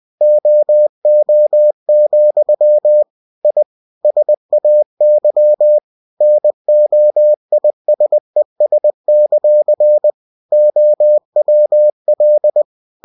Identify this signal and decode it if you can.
Morse code, heard as a keyed tone, about 20 wpm, 600 Hz. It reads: OO, I SAY NOISES; OWL
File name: winnie-20wpm